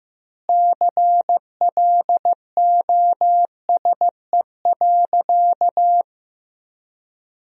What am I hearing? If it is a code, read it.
Morse code, heard as a keyed tone, about 16 wpm, 700 Hz. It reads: CLOSE.
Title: 3x Word-Spacing - Sentences from Top 300 Words - 15wpm